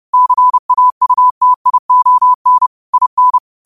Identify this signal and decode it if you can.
Morse code, heard as a keyed tone, about 30 wpm, 1000 Hz. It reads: CAUTION IN
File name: qotdfiles / 30wpm